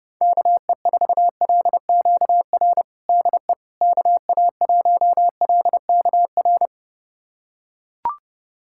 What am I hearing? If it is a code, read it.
Morse code, heard as a keyed tone, about 30 wpm, 700 Hz. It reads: KE4LQR DE KA1LKR
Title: USA Callsign DE Callsign - 6 Characters - 30wpm